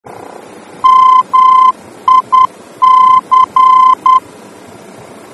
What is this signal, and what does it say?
Morse code, heard as a keyed tone, about 10 wpm, 1030 Hz. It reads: MIC
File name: germany / mx / MIC-D-VOR